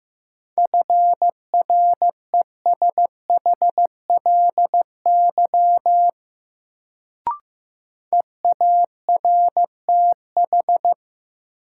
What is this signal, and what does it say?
Morse code, heard as a keyed tone, about 15 wpm, 700 Hz. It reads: FRESHLY EARTH
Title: Sets of 5 Words Encore - New Word Drill - 15wpm